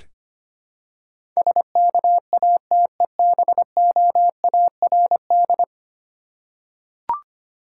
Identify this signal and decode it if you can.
Morse code, heard as a keyed tone, about 25 wpm, 700 Hz. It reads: SKATEBOARD
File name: practice / 25wpm / Difficult Words with Hints - 25wpm